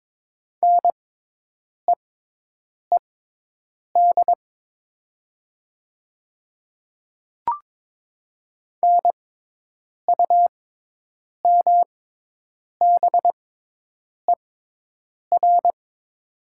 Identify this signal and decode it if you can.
Morse code, heard as a keyed tone, about 22 wpm, spaced as for 8 wpm, 700 Hz. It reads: NEED NUMBER
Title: 6x Character Spacing - Top 300 Words - 22wpm